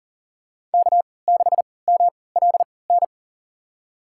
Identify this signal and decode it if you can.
Morse code, heard as a keyed tone, about 40 wpm, spaced as for 23 wpm, 700 Hz. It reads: K6MLN
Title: USA Call Signs - 5 Characters - 3x Character Spacing - 40wpm